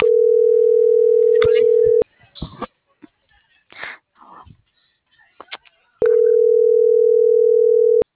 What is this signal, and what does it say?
Ringback tone (North America), 440 + 480 Hz.